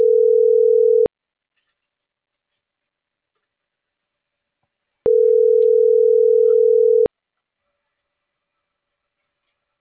Ringback tone (North America), 440 + 480 Hz.